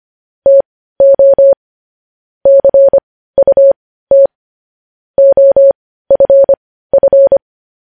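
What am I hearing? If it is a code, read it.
Morse code, heard as a keyed tone, about 25 wpm, spaced as for 15 wpm, 550 Hz. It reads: TO CUT OFF
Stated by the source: Morse code news headlines at 15 wpm.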